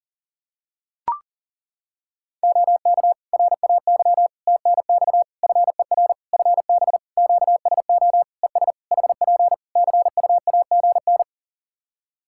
Morse code, heard as a keyed tone, about 40 wpm, 700 Hz. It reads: OK RAY TNX FER FB QSO ES HP CUAGN
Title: Practice QSO - Exchange 4 - Ending - 40wpm